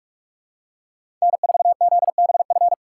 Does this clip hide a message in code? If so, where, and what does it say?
Yes: Morse code from 1.2 to 2.7 s, heard as a keyed tone, about 46 wpm, 700 Hz. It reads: N4ZBF